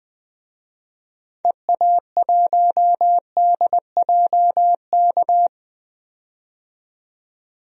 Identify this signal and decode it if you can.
Morse code, heard as a keyed tone, about 20 wpm, 700 Hz. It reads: EA1DJK
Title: DXCC-100 Call Signs - 6 Characters - 20wpm